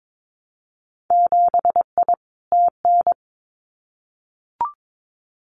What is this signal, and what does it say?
Morse code, heard as a keyed tone, about 22 wpm, 700 Hz. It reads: 7I TN
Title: Winter Field Day Exchanges Encore 22wpm